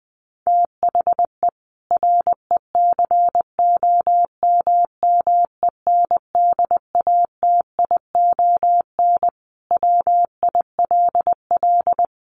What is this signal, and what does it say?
Morse code, heard as a keyed tone, about 20 wpm, 700 Hz. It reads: THE RECOMMENDATION WILL
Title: Sentences from Top 1400 Words 20wpm